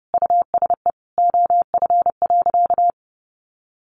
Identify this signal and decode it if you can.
Morse code, heard as a keyed tone, about 30 wpm, 700 Hz. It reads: USE OF.
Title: Sentences from Top 1200 Words 30wpm